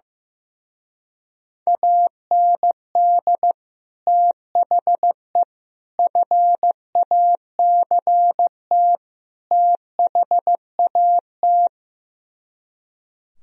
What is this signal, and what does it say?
Morse code, heard as a keyed tone, about 15 wpm, 700 Hz. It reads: AND THE FACT THAT
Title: Speed-Racing - Sets of 4 Words 15wpm